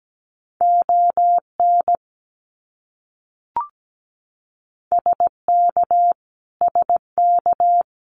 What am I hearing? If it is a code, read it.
Morse code, heard as a keyed tone, about 17 wpm, 700 Hz. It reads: ON SK SK
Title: Canadian Provinces and Territories - Repeated 3x - 17wpm